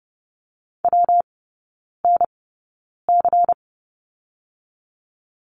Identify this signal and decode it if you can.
Morse code, heard as a keyed tone, about 30 wpm, spaced as for 9 wpm, 700 Hz. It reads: WNC
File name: Lesson 19e - Letter w - Three Characters 30wpm